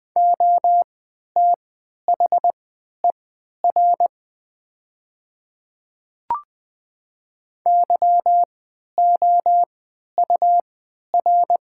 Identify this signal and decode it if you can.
Morse code, heard as a keyed tone, about 20 wpm, spaced as for 11 wpm, 700 Hz. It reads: OTHER YOUR
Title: 3x Character Spacing - Top 100 Words - 20wpm